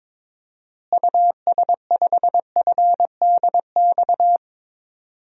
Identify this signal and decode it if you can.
Morse code, heard as a keyed tone, about 22 wpm, 700 Hz. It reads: US5FDX
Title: DXCC-100 Call Signs - 6 Characters - Repeated 2x - 22wpm